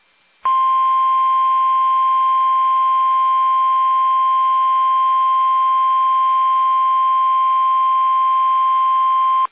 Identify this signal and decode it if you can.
Emergency Alert System (SAME) tones (the 1050 Hz attention tone). It is heard from 0.4 to 8.4 s.
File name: eas / KUSO / KUSO